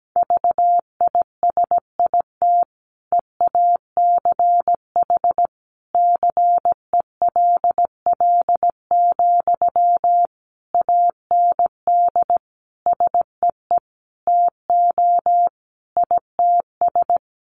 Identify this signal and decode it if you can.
Morse code, heard as a keyed tone, about 17 wpm, 700 Hz. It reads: VISIT EACH CELL, AND SEE TO ITS
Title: Sentences from Top 1100 Words 17wpm